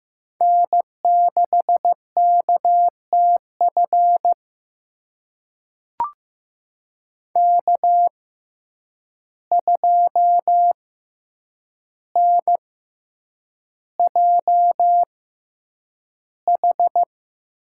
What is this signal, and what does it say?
Morse code, heard as a keyed tone, about 15 wpm, spaced as for 8 wpm, 700 Hz. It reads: N6KTF K2NJH
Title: USA Call Signs - 5 Characters - 6x Character Spacing - 15wpm